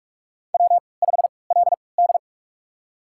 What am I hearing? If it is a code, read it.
Morse code, heard as a keyed tone, about 46 wpm, spaced as for 26 wpm, 700 Hz. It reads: W5LD
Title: USA Call Signs - 4 Characters - 3x Character Spacing - 45wpm